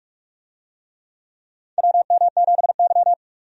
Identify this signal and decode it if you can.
Morse code, heard as a keyed tone, about 43 wpm, 700 Hz. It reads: WM7Y
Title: USA Call Signs - 4 Characters - 45wpm